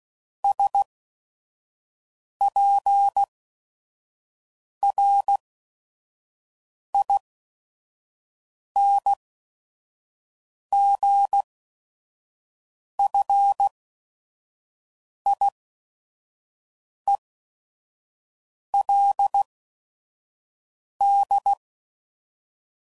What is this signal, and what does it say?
Morse code, heard as a keyed tone, about 16 wpm, spaced as for 5 wpm, 800 Hz. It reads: SPRINGFIELD
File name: CW Code Groups @5WPM / fromSM's / Cities